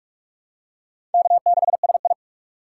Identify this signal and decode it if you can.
Morse code, heard as a keyed tone, about 41 wpm, 700 Hz. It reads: K6SI
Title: USA Call Signs - 4 Characters - 4x Character Spacing - 45wpm